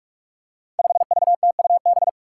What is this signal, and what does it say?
Morse code, heard as a keyed tone, about 46 wpm, 700 Hz. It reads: 5VTUB
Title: Five Challenging Letter-Number Combinations 45wpm